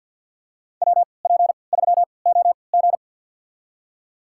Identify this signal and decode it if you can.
Morse code, heard as a keyed tone, about 52 wpm, spaced as for 28 wpm, 700 Hz. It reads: WP3OG
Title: USA Call Signs - 5 Characters - 3x Character Spacing - 50wpm